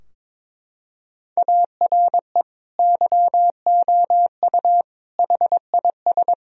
Morse code, heard as a keyed tone, about 22 wpm, 700 Hz. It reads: ARE YOU HIS
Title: Sentences from Top 1000 Words - Review Part 3 - 22wpm